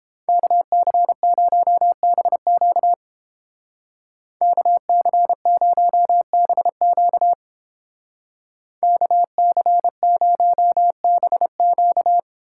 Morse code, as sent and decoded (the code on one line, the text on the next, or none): -.- -.-. ----- -... --.- / -.- -.-. ----- -... --.- / -.- -.-. ----- -... --.-
KC0BQ KC0BQ KC0BQ